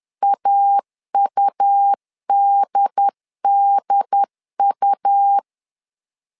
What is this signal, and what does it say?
Morse code, heard as a keyed tone, about 11 wpm, 800 Hz. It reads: AUDDU
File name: LEZIONI / 3 MWRO / 3 M 10wpm 1828